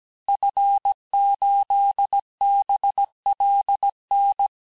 Morse code, heard as a keyed tone, about 17 wpm, 800 Hz. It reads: F8BLN
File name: corrige-cours-1581-fin-du-QSO-vitesse-16-mpm